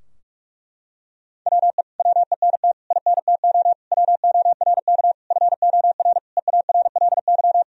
Morse code, heard as a keyed tone, about 46 wpm, 700 Hz. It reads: WE WENT INTO WORK FOR EARLY
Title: Sentences from Top 1000 Words - Review Part 4 - 45wpm